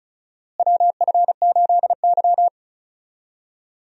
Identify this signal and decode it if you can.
Morse code, heard as a keyed tone, about 36 wpm, 700 Hz. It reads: WF8Y
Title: USA Call Signs - 4 Characters - Repeated 2x - 35wpm